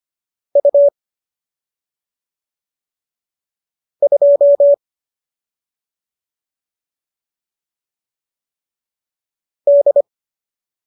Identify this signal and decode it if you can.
Morse code, heard as a keyed tone, about 25 wpm, 570 Hz. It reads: U 2 D